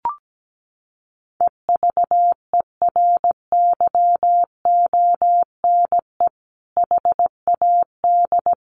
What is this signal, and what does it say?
Morse code, heard as a keyed tone, about 17 wpm, 700 Hz. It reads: EVERYONE HAD
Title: Sentences from Top 1500 Words 17wpm